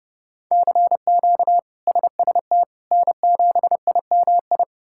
Morse code, heard as a keyed tone, about 30 wpm, 700 Hz. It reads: CQ SST N7IMI
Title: K1USN SST QSO Practice - 30wpm